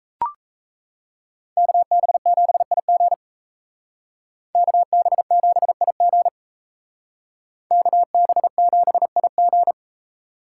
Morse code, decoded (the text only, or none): KB7IG KB7IG KB7IG